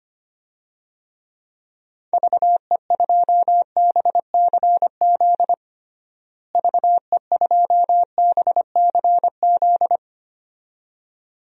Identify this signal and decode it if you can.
Morse code, heard as a keyed tone, about 25 wpm, 700 Hz. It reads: VE2BCZ VE2BCZ
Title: DXCC-100 Call Signs - 6 Characters - Repeated 2x - 25wpm